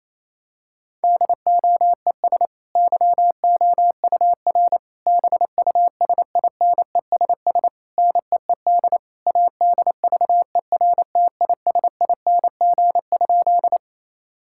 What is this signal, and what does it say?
Morse code, heard as a keyed tone, about 28 wpm, 700 Hz. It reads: DOES YOUR BUSINESS NEED ADVERTISING?
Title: Sentences from Top 1000 Words - Review Part 6 - 28wpm